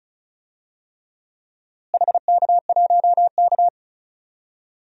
Morse code, heard as a keyed tone, about 36 wpm, 700 Hz. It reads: HK1K
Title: DXCC-100 Call Signs - 4 Characters - Repeated 3x - 35wpm